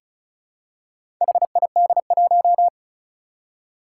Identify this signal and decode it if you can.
Morse code, heard as a keyed tone, about 34 wpm, 700 Hz. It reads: HID1
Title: Four Challenging Letter-Number Combinations 35wpm